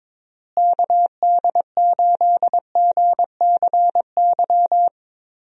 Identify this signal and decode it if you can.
Morse code, heard as a keyed tone, about 22 wpm, 700 Hz. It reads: KD8GCY
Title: USA Call Signs - 6 Characters - Repeated 3x - 22wpm